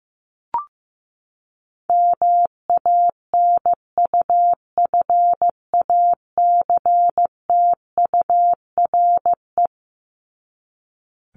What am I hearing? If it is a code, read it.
Morse code, heard as a keyed tone, about 15 wpm, 700 Hz. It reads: MANUFACTURE
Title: ING Suffix - Top 100 Words with Base Words 15wpm